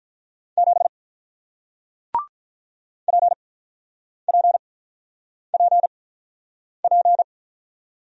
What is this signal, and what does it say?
Morse code, heard as a keyed tone, about 48 wpm, spaced as for 9 wpm, 700 Hz. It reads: 7 PPPP